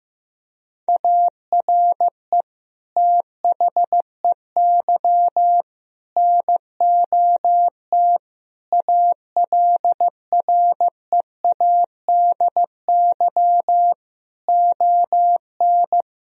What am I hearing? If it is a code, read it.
Morse code, heard as a keyed tone, about 15 wpm, 700 Hz. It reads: ARE THEY NOT ALREADY ON
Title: Sentences from Top 1000 Words - Review Part 1 - 15wpm